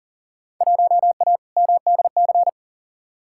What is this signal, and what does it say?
Morse code, heard as a keyed tone, about 38 wpm, 700 Hz. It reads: 1A MDC
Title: Field Day Exchanges - 40wpm